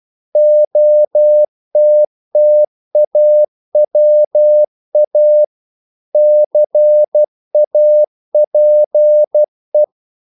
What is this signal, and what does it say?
Morse code, heard as a keyed tone, about 12 wpm, 600 Hz. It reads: OTTAWA CAPE
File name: CW / Practice / NationalCapitals12